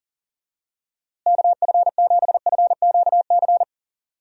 Morse code, heard as a keyed tone, about 40 wpm, 700 Hz. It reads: KF7FQC
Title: USA Call Signs - 6 Characters - Repeated 2x - 40wpm